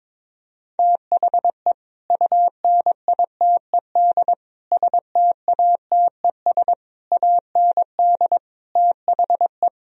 Morse code, heard as a keyed tone, about 22 wpm, 700 Hz. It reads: THE UNITED STATES AND THE
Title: Speed-Racing - Sets of 5 Words 22wpm